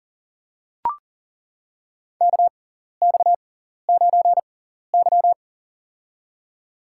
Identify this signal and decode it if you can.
Morse code, heard as a keyed tone, about 40 wpm, spaced as for 14 wpm, 700 Hz. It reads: KX9Y
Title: USA Call Signs - 4 Characters - 6x Character Spacing - 40wpm